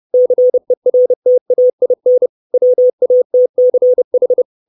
Morse code, heard as a keyed tone, about 30 wpm, 500 Hz. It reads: CERTAIN WATCH
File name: 632 Words 30WPM